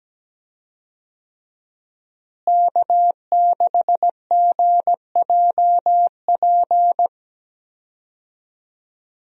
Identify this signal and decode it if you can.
Morse code, heard as a keyed tone, about 17 wpm, 700 Hz. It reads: K6GJP
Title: USA Call Signs - 5 Characters - Repeated 2x - 17wpm